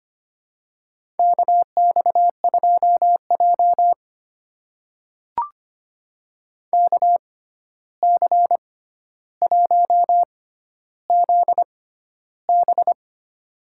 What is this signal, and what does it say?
Morse code, heard as a keyed tone, about 25 wpm, spaced as for 12 wpm, 700 Hz. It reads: KX2J KC1ZB
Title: USA Call Signs - 6x Character Spacing - 25wpm